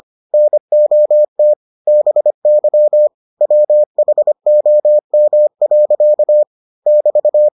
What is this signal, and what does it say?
Morse code, heard as a keyed tone, about 25 wpm, 600 Hz. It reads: NOT BY WHOM. <BT>